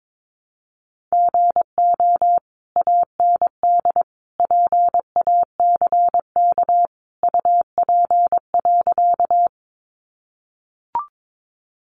Morse code, heard as a keyed tone, about 22 wpm, 700 Hz. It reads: GO AND PACK UP.
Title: Sentences from Top 1300 Words 22wpm